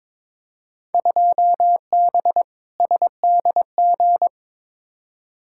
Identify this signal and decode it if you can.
Morse code, heard as a keyed tone, about 22 wpm, 700 Hz. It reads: 2B SDG
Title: Field Day Exchanges Encore - 22wpm